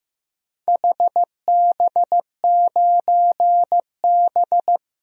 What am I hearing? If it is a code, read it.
Morse code, heard as a keyed tone, about 15 wpm, 700 Hz. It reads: HB9B
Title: DXCC-100 Call Signs - 4 Characters - 15wpm